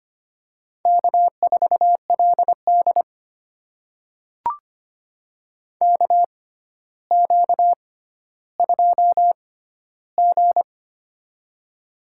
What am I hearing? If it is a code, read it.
Morse code, heard as a keyed tone, about 25 wpm, spaced as for 13 wpm, 700 Hz. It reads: K4LD KQ2G
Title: USA Call Signs - 4 Characters - 6x Character Spacing - 25wpm